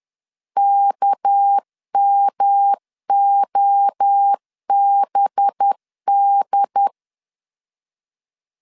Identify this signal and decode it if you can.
Morse code, heard as a keyed tone, about 10 wpm, 800 Hz. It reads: KMOBD
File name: LEZIONI / 8 YJ45 / 8 YJ 10wpm 2651